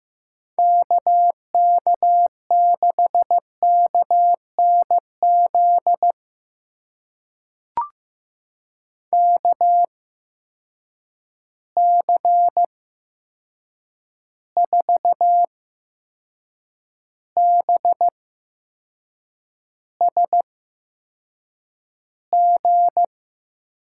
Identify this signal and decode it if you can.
Morse code, heard as a keyed tone, about 15 wpm, spaced as for 6 wpm, 700 Hz. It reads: KK6KNZ KC4BSG
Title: USA Call Signs - 6 Characters - 8x Character Spacing - 15wpm